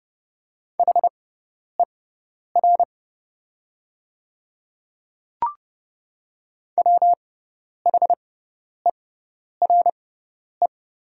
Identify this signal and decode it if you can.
Morse code, heard as a keyed tone, about 30 wpm, spaced as for 10 wpm, 700 Hz. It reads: HER WHERE
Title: 6x Character Spacing - Top 300 Words - 30wpm